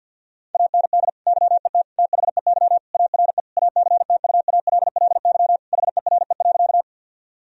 Morse code, heard as a keyed tone, about 50 wpm, 700 Hz. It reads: AND YET THEY ARE ACTUALLY HERE.
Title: Sentences from Top 1000 Words - Review Part 4 - 50wpm